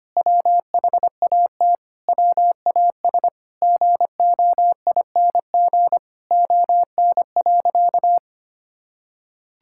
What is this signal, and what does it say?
Morse code, heard as a keyed tone, about 25 wpm, 700 Hz. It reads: WHAT WAS GOING ON.
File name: practice / 25wpm / Sentences with 4-Word Phrases - 25wpm